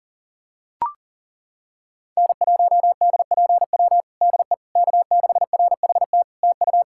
Morse code, heard as a keyed tone, about 40 wpm, 700 Hz. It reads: N1DPW DE K6RHT TU
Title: Practice QSO - Exchange 4 - Ending - 40wpm